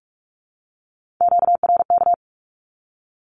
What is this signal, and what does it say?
Morse code, heard as a keyed tone, about 44 wpm, 700 Hz. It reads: QRK
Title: Q-Codes Repeated 3x 45wpm